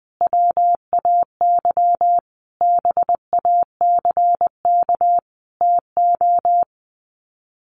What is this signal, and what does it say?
Morse code, heard as a keyed tone, about 20 wpm, 700 Hz. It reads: WAY BACK TO
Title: Sets of 5 Words Encore 20wpm